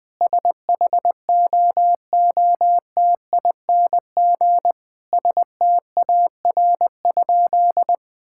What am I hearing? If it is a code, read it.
Morse code, heard as a keyed tone, about 20 wpm, 700 Hz. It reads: SHOOTING STAR?
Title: Sentences with 8-Word Phrases - 20wpm